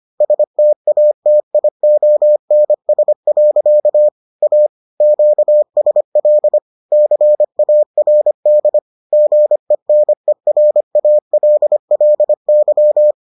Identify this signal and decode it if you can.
Morse code, heard as a keyed tone, about 25 wpm, 600 Hz. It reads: STATIONS. A QSL CARD GENERALLY